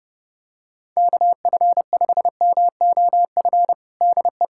Morse code, heard as a keyed tone, about 30 wpm, 700 Hz. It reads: KF5MOF DE
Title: USA Callsign DE Callsign - 6 Characters - 30wpm